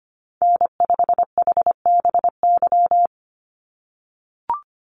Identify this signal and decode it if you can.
Morse code, heard as a keyed tone, about 25 wpm, 700 Hz. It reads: N5HBY
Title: USA Call Signs 25wpm